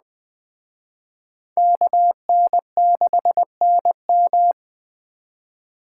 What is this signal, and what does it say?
Morse code, heard as a keyed tone, about 20 wpm, 700 Hz. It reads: KN6NM